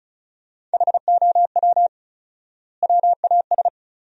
Morse code, heard as a keyed tone, about 36 wpm, 700 Hz. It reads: HOW WAS
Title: 4x Word-Spacing - Sentences from Top 100 Words - 35wpm